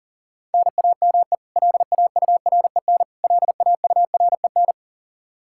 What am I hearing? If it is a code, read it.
Morse code, heard as a keyed tone, about 39 wpm, 700 Hz. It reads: NAME LAUREN LAUREN
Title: Practice QSO - Exchange 1 - Beginning - 40wpm